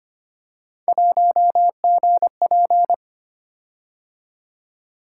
Morse code, heard as a keyed tone, about 25 wpm, 700 Hz. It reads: 1GP
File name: Three Letter-Number Combinations - 25wpm